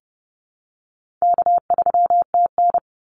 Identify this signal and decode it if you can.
Morse code, heard as a keyed tone, about 30 wpm, 700 Hz. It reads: K3TN
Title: Easy Call Signs - 30wpm